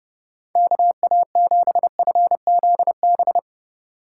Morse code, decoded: KA7FZB

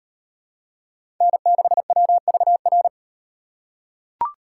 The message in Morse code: N6WVR

